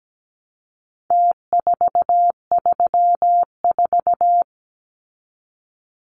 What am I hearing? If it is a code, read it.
Morse code, heard as a keyed tone, about 17 wpm, 700 Hz. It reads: T434
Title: Numbers - Contest Style - 4-Digit 17wpm